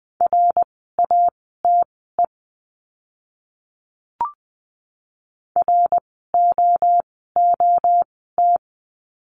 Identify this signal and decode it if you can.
Morse code, heard as a keyed tone, about 20 wpm, spaced as for 14 wpm, 700 Hz. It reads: RATE ROOT
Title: 054 Lesson 10f - Letter r - Words 20wpm